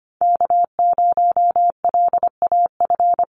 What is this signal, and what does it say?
Morse code, heard as a keyed tone, about 25 wpm, 700 Hz. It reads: K0LAF
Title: NA SOTA Chaser Callsigns 25wpm